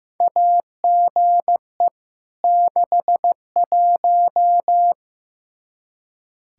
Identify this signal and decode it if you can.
Morse code, heard as a keyed tone, about 15 wpm, 700 Hz. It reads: AGE 61
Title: K1USN SST QSO Practice 15wpm